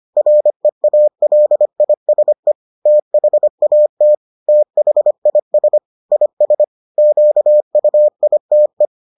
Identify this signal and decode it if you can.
Morse code, heard as a keyed tone, about 25 wpm, 600 Hz. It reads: REALISE THAT THIS IS QUITE